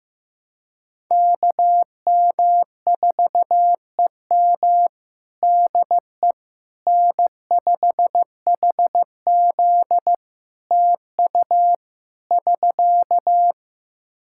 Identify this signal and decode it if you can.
Morse code, heard as a keyed tone, about 15 wpm, 700 Hz. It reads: KM4EM DE N5HZ TU <SK>
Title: Practice QSO - Exchange 4 - Ending - 15wpm